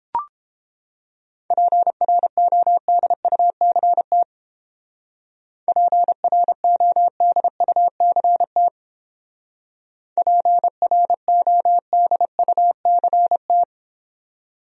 Morse code, heard as a keyed tone, about 31 wpm, 700 Hz. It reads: PRODUCT PRODUCT PRODUCT